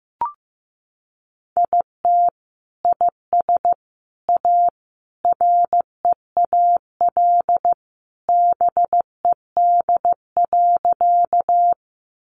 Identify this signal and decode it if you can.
Morse code, heard as a keyed tone, about 15 wpm, 700 Hz. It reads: IT IS A REAL BED.
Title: Sentences from Top 500 Words 15wpm